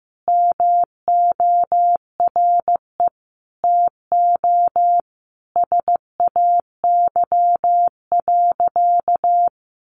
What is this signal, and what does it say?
Morse code, heard as a keyed tone, about 15 wpm, 700 Hz. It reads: MORE TO SAY.
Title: Sentences from Top 500 Words 15wpm